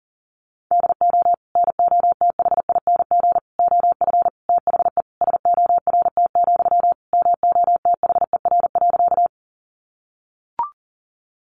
Morse code, heard as a keyed tone, about 40 wpm, 700 Hz. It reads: DO NOTHING OF THE SORT, MOTHER.
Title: Sentences from Top 1300 Words 40wpm